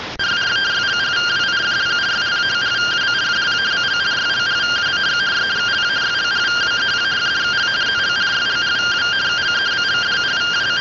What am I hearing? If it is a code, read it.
Two-tone teleprinter signal, about 170 Hz shift.